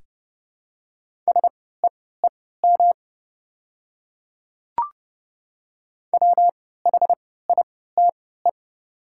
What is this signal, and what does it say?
Morse code, heard as a keyed tone, about 29 wpm, spaced as for 17 wpm, 700 Hz. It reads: SEEM WHITE
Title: 3x Character Spacing - Top 300 Words - 30wpm